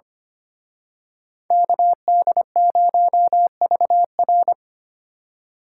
Morse code, heard as a keyed tone, about 31 wpm, 700 Hz. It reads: KD0VR
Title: Speed Racing - USA 2x2 Call Signs 25wpm